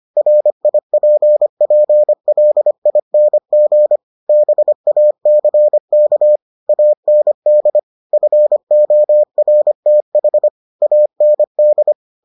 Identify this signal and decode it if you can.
Morse code, heard as a keyed tone, about 25 wpm, 600 Hz. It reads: RIPPLING BACK AND FORTH AND